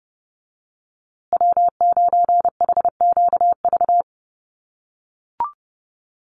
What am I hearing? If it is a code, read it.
Morse code, heard as a keyed tone, about 30 wpm, 700 Hz. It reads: W9HQV
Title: USA Call Signs Repeated 2x 30wpm